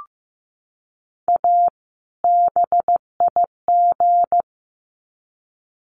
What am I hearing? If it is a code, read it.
Morse code, heard as a keyed tone, about 15 wpm, 700 Hz. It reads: A BIG
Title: Sets of 2 Words 15wpm